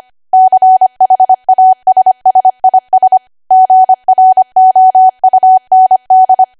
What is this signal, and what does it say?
Morse code, heard as a keyed tone, about 25 wpm, 750 Hz. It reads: CHASSIS GROUND
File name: Archive / 25 WPM / 25WPM